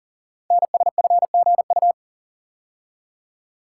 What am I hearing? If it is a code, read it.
Morse code, heard as a keyed tone, about 39 wpm, 700 Hz. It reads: NSFGU
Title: Code Groups - 5 Letters - 40wpm